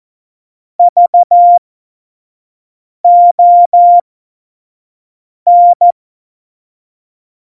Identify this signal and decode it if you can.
Morse code, heard as a keyed tone, about 14 wpm, spaced as for 5 wpm, 700 Hz. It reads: VON
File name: Morsekurs / Witze / Witz 23. Juli 2019, 25 F70